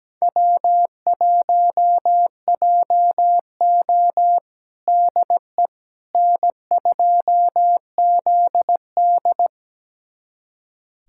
Morse code, heard as a keyed tone, about 17 wpm, 700 Hz. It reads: W1JO DE N2ZD
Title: USA Callsign DE Callsign - 4 Characters - 17wpm